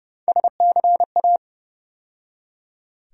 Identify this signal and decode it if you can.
Morse code, heard as a keyed tone, about 30 wpm, 700 Hz. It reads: SCA
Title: USA Call Sign Postfixes - 3 Letters 30wpm